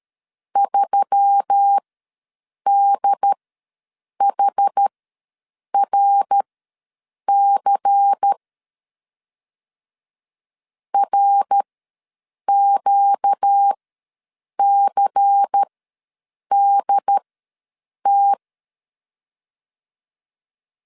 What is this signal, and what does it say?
Morse code, heard as a keyed tone, about 13 wpm, spaced as for 7 wpm, 800 Hz. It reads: 3DHRC RQCDT
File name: LEZIONI / 7 GF9Z / 7 GF9 ws 12wpm 3344